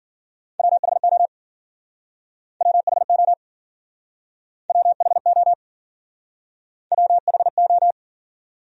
Morse code, heard as a keyed tone, about 52 wpm, 700 Hz. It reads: WHO WHO WHO WHO